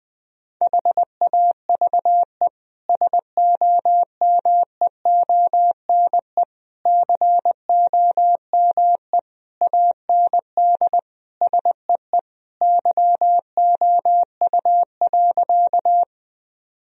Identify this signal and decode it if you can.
Morse code, heard as a keyed tone, about 20 wpm, 700 Hz. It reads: HAVE SOMEONE COME AND SEE YOU.
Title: Sentences from Top 1000 Words - Review Part 4 - 20wpm